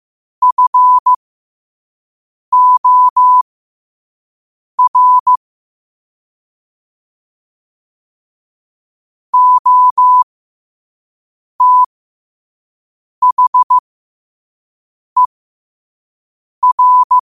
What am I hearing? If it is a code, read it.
Morse code, heard as a keyed tone, about 15 wpm, spaced as for 5 wpm, 1000 Hz. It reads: FOR OTHER
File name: qotdfiles / 5wpm